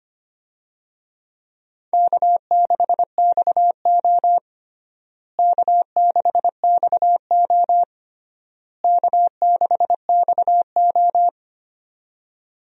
Morse code, heard as a keyed tone, about 25 wpm, 700 Hz. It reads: K6XO K6XO K6XO
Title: USA Call Signs - 4 Characters - Repeated 3x - 25wpm